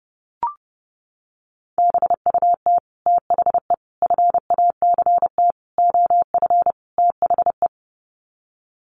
Morse code, heard as a keyed tone, about 28 wpm, 700 Hz. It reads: BUT THE FACT OF THE
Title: Sets of 5 Words Encore2 30wpm